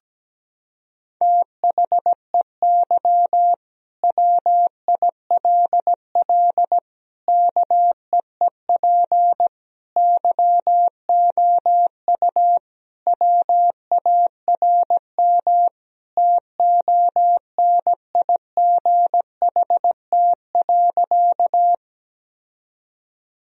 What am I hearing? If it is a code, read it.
Morse code, heard as a keyed tone, about 17 wpm, 700 Hz. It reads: THEY WILL KEEP YOU WARM TONIGHT.
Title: Sentences from Top 1000 Words - Review Part 3 - 17wpm